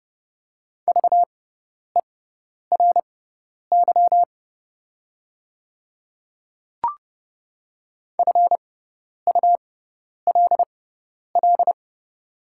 Morse code, heard as a keyed tone, about 32 wpm, spaced as for 10 wpm, 690 Hz. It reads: VERY FULL
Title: 6x Character Spacing - Top 500 Words - 30wpm